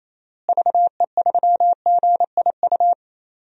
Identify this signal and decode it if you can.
Morse code, heard as a keyed tone, about 28 wpm, 700 Hz. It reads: VE3GIU
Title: DXCC-100 Call Signs - 6 Characters - Repeated 3x - 28wpm